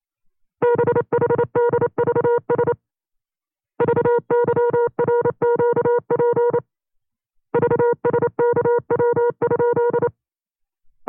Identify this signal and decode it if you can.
Morse code, heard as a keyed tone, about 28 wpm, 480 Hz. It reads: BHDVS VYRQP VSKW?
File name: CW-practice/25WPM